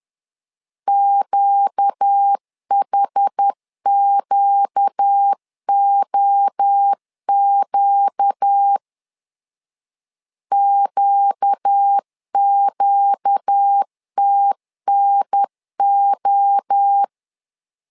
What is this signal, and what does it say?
Morse code, heard as a keyed tone, about 10 wpm, 800 Hz. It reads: QHQOQ QQTNO
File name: LEZIONI / 4 Q 10wpm 2210